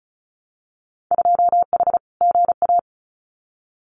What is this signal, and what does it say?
Morse code, heard as a keyed tone, about 34 wpm, 700 Hz. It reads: 2H GA
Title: Winter Field Day Exchanges 35wpm